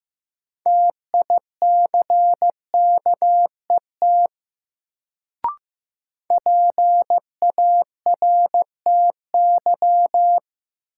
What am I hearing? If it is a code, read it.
Morse code, heard as a keyed tone, about 15 wpm, 700 Hz. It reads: TICKET PARTY
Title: Sentences from Top 1400 Words - New Word Drill Alt - 15wpm